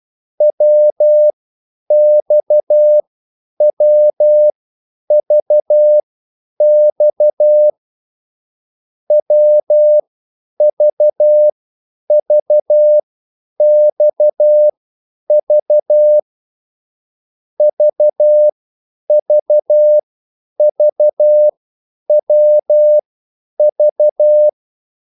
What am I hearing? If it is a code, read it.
Morse code, heard as a keyed tone, about 12 wpm, spaced as for 9 wpm, 600 Hz. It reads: WXWVX WVVXV VVVWV
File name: QSL / CW lesson